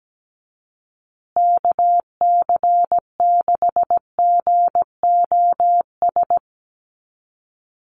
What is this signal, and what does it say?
Morse code, heard as a keyed tone, about 17 wpm, 700 Hz. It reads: KC6GOS
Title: USA Call Signs Repeated 3x 17wpm